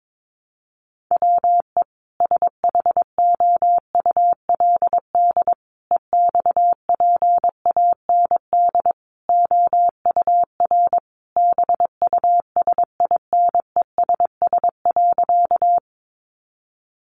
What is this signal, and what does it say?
Morse code, heard as a keyed tone, about 22 wpm, 700 Hz. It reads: WE SHOULD EXPAND OUR BUSINESS.
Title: Sentences from Top 1400 Words 22wpm